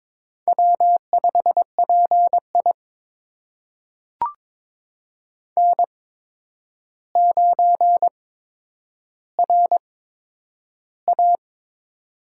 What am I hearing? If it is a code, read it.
Morse code, heard as a keyed tone, about 22 wpm, spaced as for 9 wpm, 700 Hz. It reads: W5PI N9RA
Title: USA Call Signs - 4 Characters - 8x Character Spacing - 22wpm